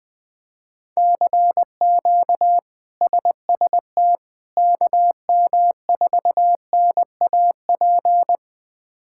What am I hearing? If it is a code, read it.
Morse code, heard as a keyed tone, about 20 wpm, 700 Hz. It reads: CQ SST KM4NAP
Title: K1USN SST QSO Practice - 20wpm